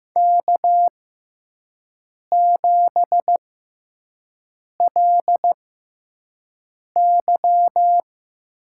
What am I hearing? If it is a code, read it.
Morse code, heard as a keyed tone, about 15 wpm, spaced as for 5 wpm, 700 Hz. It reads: K7LY
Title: USA Call Signs - 6x Character Spacing - 15wpm